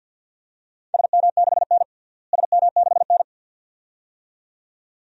Morse code, heard as a keyed tone, about 47 wpm, 700 Hz. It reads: SM6N SM6N
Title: DXCC-100 Call Signs - 4 Characters - Repeated 2x - 50wpm